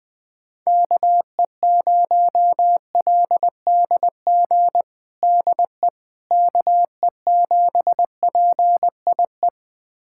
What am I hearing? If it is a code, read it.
Morse code, heard as a keyed tone, about 20 wpm, 700 Hz. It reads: KE0LDG DE KE7PIE
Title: USA Callsign DE Callsign - 6 Characters - 20wpm